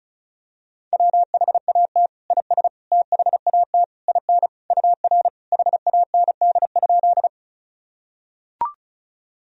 Morse code, heard as a keyed tone, about 36 wpm, 700 Hz. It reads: WHAT IS THAT IN UR HAND?
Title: Sentences from Top 200 Words and Abbreviations 35wpm